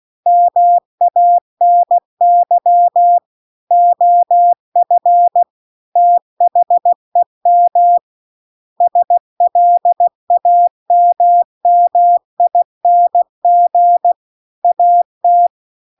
Morse code, heard as a keyed tone, about 16 wpm, 700 Hz. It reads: MANY OF THEM SLAMMING AT
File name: CW-The-Big-Twoheaded-River-2-16WPM